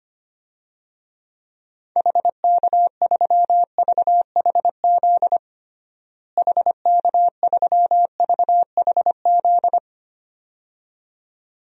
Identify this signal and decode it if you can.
Morse code, heard as a keyed tone, about 25 wpm, 700 Hz. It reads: HK3VHZ HK3VHZ
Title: DXCC-100 Call Signs - 6 Characters - Repeated 2x - 25wpm